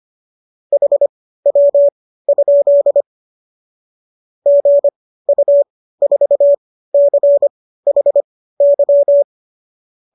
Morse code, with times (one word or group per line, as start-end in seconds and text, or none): 0.7-3.0 HW?
4.5-9.2 GU4CHY